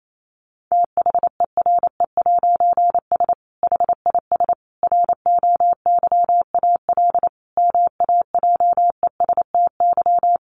Morse code, heard as a keyed tone, about 28 wpm, 700 Hz. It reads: THERE'S HIS ROYAL MAJESTY
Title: A Sound Of Thunder 28wpm